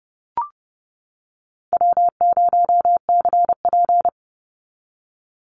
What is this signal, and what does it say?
Morse code, heard as a keyed tone, about 30 wpm, 700 Hz. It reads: W0CP
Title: NA SOTA Activator Callsigns 30wpm